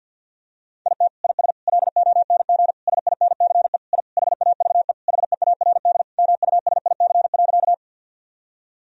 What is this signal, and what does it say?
Morse code, heard as a keyed tone, about 51 wpm, 700 Hz. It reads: IT IS LONG SINCE I HAVE HEARD MUSIC.
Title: Sentences from Top 1000 Words - Review Part 1 - 50wpm